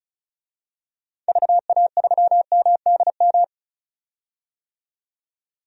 Morse code, heard as a keyed tone, about 36 wpm, 700 Hz. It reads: VA3MDM
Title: DXCC-100 Call Signs - 6 Characters - 35wpm